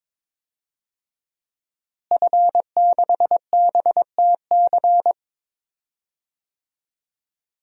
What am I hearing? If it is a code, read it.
Morse code, heard as a keyed tone, about 22 wpm, 700 Hz. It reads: F6BTC
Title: DXCC-100 Call Signs - 5 Characters - Repeated 2x - 22wpm